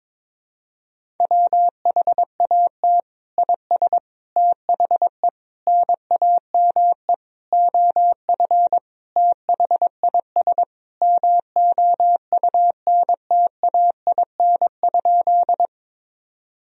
Morse code, heard as a keyed tone, about 22 wpm, 700 Hz. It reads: WHAT IS THE NAME OF THIS MOUNTAIN?
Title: Sentences from Top 1000 Words - Review Part 1 - 22wpm